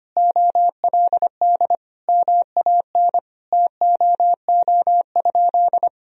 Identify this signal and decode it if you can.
Morse code, heard as a keyed tone, about 25 wpm, 700 Hz. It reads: OLD MAN TOO?
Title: Sentences from Top 1000 Words - Review Part 4 - 25wpm